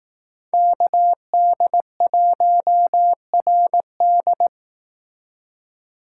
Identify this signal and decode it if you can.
Morse code, heard as a keyed tone, about 18 wpm, 700 Hz. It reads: KD1RD
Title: Speed Racing - USA 2x2 Call Signs 15wpm